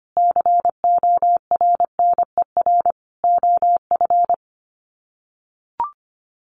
Morse code, heard as a keyed tone, about 25 wpm, 700 Hz. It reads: CORNER OF
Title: Sets of 5 Words 25wpm